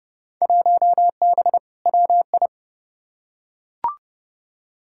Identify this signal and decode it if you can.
Morse code, heard as a keyed tone, about 30 wpm, 700 Hz. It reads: 1B WI
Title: Field Day Exchanges - 30wpm